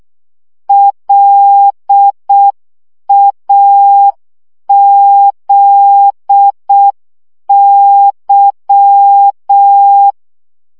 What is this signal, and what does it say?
Morse code, heard as a keyed tone, about 6 wpm, 800 Hz. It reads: LAZY
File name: CW 5 baud